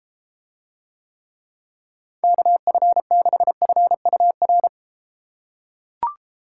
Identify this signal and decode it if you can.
Morse code, heard as a keyed tone, about 32 wpm, 700 Hz. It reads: KF6FUR